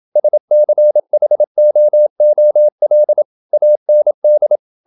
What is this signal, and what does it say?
Morse code, heard as a keyed tone, about 27 wpm, 600 Hz. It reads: SCHOOL AND
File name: Never 0027wpm